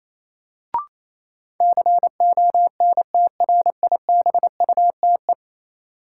Morse code, heard as a keyed tone, about 28 wpm, 700 Hz. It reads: CONTRIBUTE
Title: Sentences from Top 1400 Words - New Word Drill Alt - 28wpm